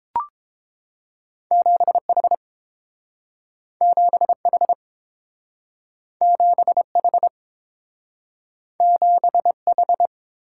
Morse code, heard as a keyed tone, about 30 wpm, 700 Hz. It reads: 7H 7H 7H 7H